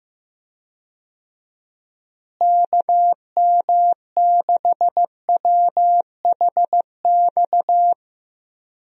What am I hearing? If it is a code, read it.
Morse code, heard as a keyed tone, about 16 wpm, 700 Hz. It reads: KM6WHX
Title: USA Call Signs - 6 Characters - Repeated 3x - 15wpm